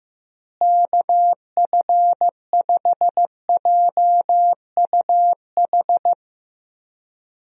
Morse code, heard as a keyed tone, about 15 wpm, 700 Hz. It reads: KF5JUH
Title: USA Call Signs - 6 Characters - Repeated 2x - 15wpm